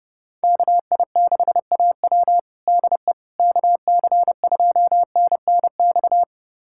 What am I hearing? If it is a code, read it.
Morse code, heard as a keyed tone, about 30 wpm, 700 Hz. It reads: KI6AW DE KC2NNX
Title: USA Callsign DE Callsign 30wpm